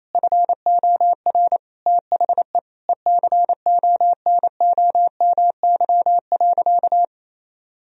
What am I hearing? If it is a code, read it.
Morse code, heard as a keyed tone, about 28 wpm, 700 Hz. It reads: FOR THE ECONOMY.
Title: Sentences from Top 1000 Words - Review Part 4 - 28wpm